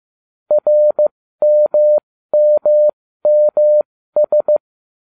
Morse code, heard as a keyed tone, about 15 wpm, spaced as for 13 wpm, 600 Hz. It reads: RMMMS